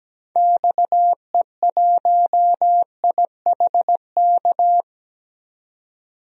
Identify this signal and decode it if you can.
Morse code, heard as a keyed tone, about 17 wpm, 700 Hz. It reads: XE1IHK
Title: DXCC-100 Call Signs - 6 Characters - 17wpm